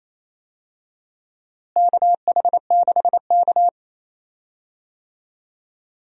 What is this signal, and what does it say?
Morse code, heard as a keyed tone, about 28 wpm, 700 Hz. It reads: KH6K
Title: DXCC-100 Call Signs - 4 Characters - Repeated 3x - 28wpm